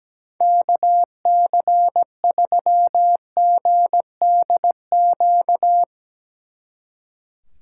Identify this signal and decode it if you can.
Morse code, heard as a keyed tone, about 17 wpm, 700 Hz. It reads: KC3GDQ
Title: Speed Racing - USA 2x3 Call Signs 17wpm